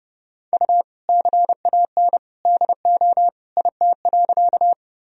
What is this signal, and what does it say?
Morse code, heard as a keyed tone, about 30 wpm, 700 Hz. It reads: U CAN DO IT.
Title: Sentences from Top 100 Words and Abbreviations 30wpm